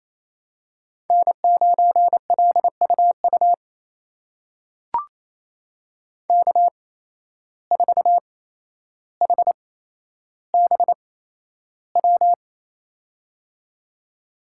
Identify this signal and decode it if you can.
Morse code, heard as a keyed tone, about 28 wpm, spaced as for 12 wpm, 700 Hz. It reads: N9LUU K4HBW
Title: USA Call Signs - 5 Characters - 8x Character Spacing - 28wpm